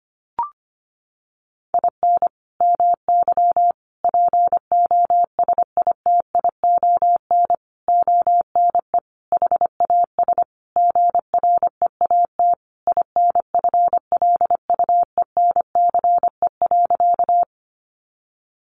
Morse code, heard as a keyed tone, about 25 wpm, 700 Hz. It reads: IN MY POSITION ONE HAS GREAT INFLUENCE.
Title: Sentences from Top 1100 Words 25wpm